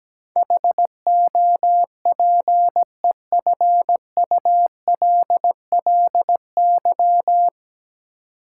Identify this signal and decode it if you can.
Morse code, heard as a keyed tone, about 17 wpm, 700 Hz. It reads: HOPEFULLY
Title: Sentences from Top 1500 Words - New Word Drill Alt - 17wpm